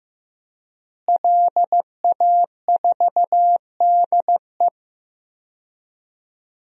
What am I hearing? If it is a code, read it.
Morse code, heard as a keyed tone, about 15 wpm, 700 Hz. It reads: LA4DE
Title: DXCC-100 Call Signs - 5 Characters - Repeated 2x - 15wpm